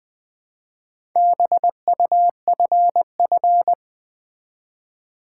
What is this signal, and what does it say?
Morse code, heard as a keyed tone, about 20 wpm, 700 Hz. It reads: BUFF
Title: Build A Word - 2 Letters At A Time - 6-Letter Words - 20wpm